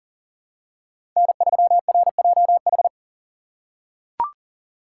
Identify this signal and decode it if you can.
Morse code, heard as a keyed tone, about 40 wpm, 690 Hz. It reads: N3RJH
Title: USA Call Signs - 3x Character Spacing - 40wpm